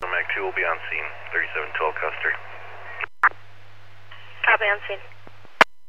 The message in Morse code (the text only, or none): none